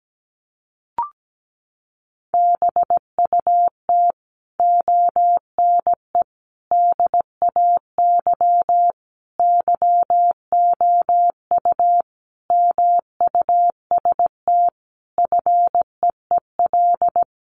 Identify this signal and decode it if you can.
Morse code, heard as a keyed tone, about 17 wpm, 700 Hz. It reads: BUT ONE DAY YOU MUST FEEL
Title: Sentences from Top 400 Words 17wpm